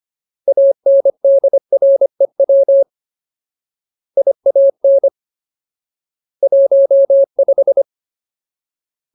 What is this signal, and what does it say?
Morse code, heard as a keyed tone, about 25 wpm, 550 Hz. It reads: ANDREW IAN 15
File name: Ultimate Word List / 25WPM Ultimate